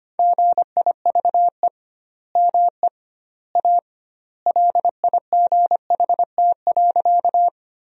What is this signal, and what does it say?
Morse code, heard as a keyed tone, about 25 wpm, 700 Hz. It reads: GIVE ME A LIGHT.
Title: 2x Word-Spacing - Sentences from Top 200 Words - 25wpm